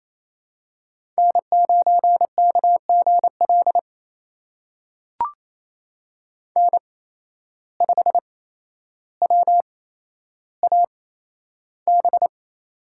Morse code, heard as a keyed tone, about 27 wpm, spaced as for 12 wpm, 700 Hz. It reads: N9KGL N5WAB
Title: USA Call Signs - 5 Characters - 8x Character Spacing - 28wpm